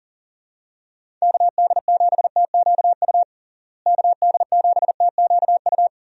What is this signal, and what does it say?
Morse code, heard as a keyed tone, about 40 wpm, 700 Hz. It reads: KD7TQU KD7TQU
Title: USA Call Signs - 6 Characters - Repeated 3x - 40wpm